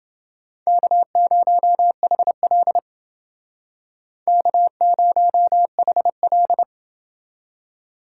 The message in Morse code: K0HL K0HL

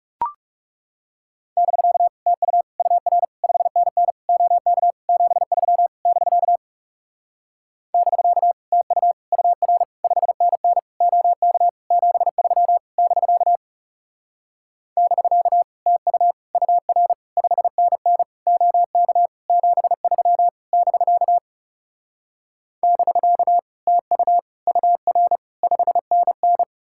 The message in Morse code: <BK> TU UR 5NN OK 73 <BK> <BK> TU UR 5NN OK 73 <BK> <BK> TU UR 5NN OK 73 <BK> <BK> TU UR 5NN

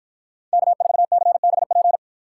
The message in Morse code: K4CBP

